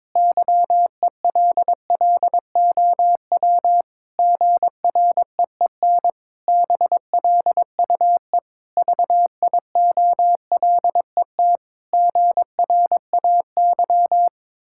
Morse code, heard as a keyed tone, about 21 wpm, 700 Hz. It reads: YELLOW GREEN BLUE VIOLET GRAY
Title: Pangram Challenge v2 - 22wpm